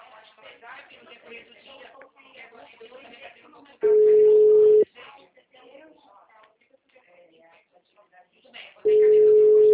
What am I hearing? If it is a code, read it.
Ringback tone (Europe, 425 Hz), with speech.